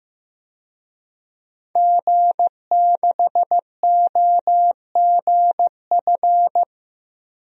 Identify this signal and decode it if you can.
Morse code, heard as a keyed tone, about 15 wpm, 700 Hz. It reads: G6OGF
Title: DXCC-100 Call Signs - 5 Characters - 15wpm